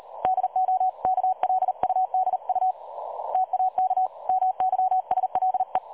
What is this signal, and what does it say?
Morse code, heard as a keyed tone, about 38 wpm, 750 Hz. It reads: DO KLUDU TAK MYSLE
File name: dlazaawansowanych / nagraniazpasma / websdr recording 3565.0kHz